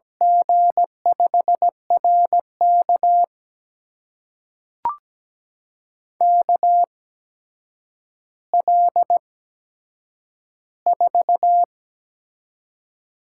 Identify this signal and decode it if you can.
Morse code, heard as a keyed tone, about 17 wpm, spaced as for 8 wpm, 700 Hz. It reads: G5RK KL4
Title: USA Call Signs - 5 Characters - 8x Character Spacing - 17wpm